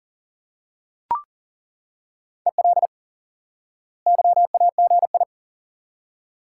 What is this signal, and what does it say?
Morse code, heard as a keyed tone, about 40 wpm, 700 Hz. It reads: EL YAGI
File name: IQE Course 3 - Lesson 89 - YAGI - Two Elements - 40wpm